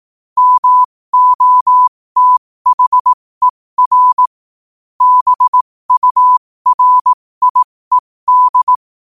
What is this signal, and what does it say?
Morse code, heard as a keyed tone, about 18 wpm, spaced as for 15 wpm, 1000 Hz. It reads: MOTHER BURIED
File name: qotdfiles / 15wpm